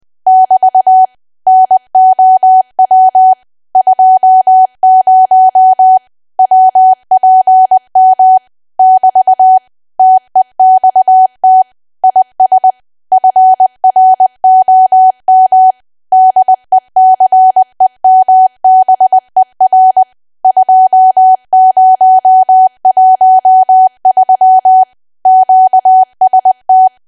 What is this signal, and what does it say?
Morse code, heard as a keyed tone, about 20 wpm, 750 Hz. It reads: <BT> NOW 20 WPM <BT> TEXT IS FROM DECEMBER 2013 QST
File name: cwa / W1AW-Level3 / 20WPM